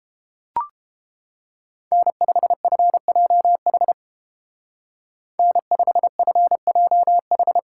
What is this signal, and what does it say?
Morse code, heard as a keyed tone, about 29 wpm, 700 Hz. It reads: N5FJH N5FJH